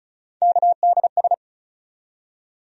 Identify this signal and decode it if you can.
Morse code, heard as a keyed tone, about 36 wpm, 700 Hz. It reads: KDS
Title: USA Call Sign Postfixes - 3 Letters 35wpm